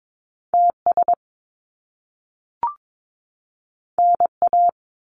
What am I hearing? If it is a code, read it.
Morse code, heard as a keyed tone, about 22 wpm, 700 Hz. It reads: TS NA
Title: Top Two Letter Combinations in Words 22wpm